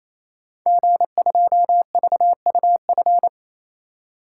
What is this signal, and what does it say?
Morse code, heard as a keyed tone, about 28 wpm, 700 Hz. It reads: G2VUF
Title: Code Groups - 5 Letter:Numbers - 28wpm